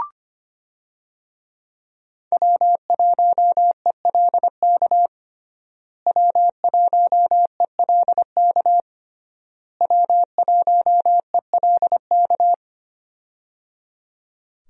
Morse code, heard as a keyed tone, about 25 wpm, 700 Hz. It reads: W1ELK W1ELK W1ELK
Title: USA Call Signs - 5 Characters - Repeated 3x - 25wpm